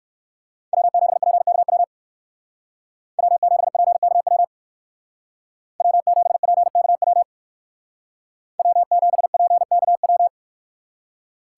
Morse code, heard as a keyed tone, about 60 wpm, 700 Hz. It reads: W7PKW W7PKW W7PKW W7PKW